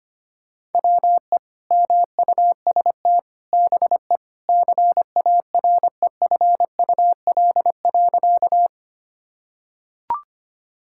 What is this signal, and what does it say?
Morse code, heard as a keyed tone, about 25 wpm, 700 Hz. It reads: WE MUST BE CAREFUL.
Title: Sentences from Top 1600 Words 25wpm